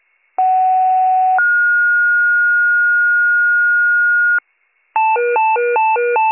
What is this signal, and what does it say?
Two-tone page, heard as a one-second tone, then a three-second tone, about 750 then 1430 Hz. It is heard from 0.4 to 4.4 s.